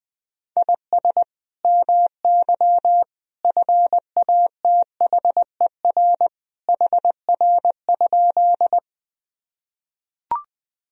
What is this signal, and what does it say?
Morse code, heard as a keyed tone, about 21 wpm, 700 Hz. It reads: IS MY FATHER HR?
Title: Sentences from Top 300 Words and Abbreviations 20wpm